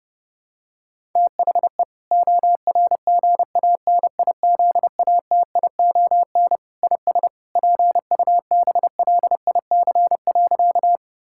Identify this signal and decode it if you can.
Morse code, heard as a keyed tone, about 31 wpm, 700 Hz. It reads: THE ORGANIZATION IS PUBLIC.
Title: Sentences from Top 1000 Words - Review Part 5 - 30wpm